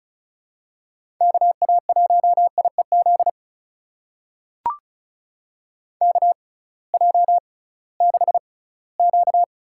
Morse code, heard as a keyed tone, about 34 wpm, spaced as for 20 wpm, 700 Hz. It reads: KA1IEZ KJ6Q
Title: USA Call Signs - 6 Characters - 6x Character Spacing - 35wpm